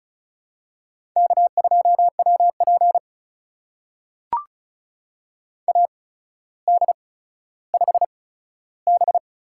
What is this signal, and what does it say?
Morse code, heard as a keyed tone, about 36 wpm, spaced as for 15 wpm, 700 Hz. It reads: K2WP AD5B
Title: USA Call Signs - 4 Characters - 8x Character Spacing - 35wpm